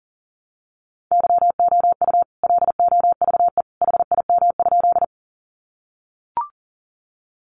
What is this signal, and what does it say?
Morse code, heard as a keyed tone, about 40 wpm, 700 Hz. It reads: YOU LOVE HIM?
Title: Sentences from Top 400 Words 40wpm